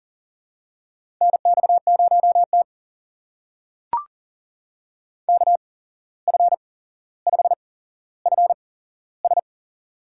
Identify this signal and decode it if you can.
Morse code, heard as a keyed tone, about 38 wpm, spaced as for 16 wpm, 700 Hz. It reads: NX0T KF5FS
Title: USA Call Signs - 8x Character Spacing - 40wpm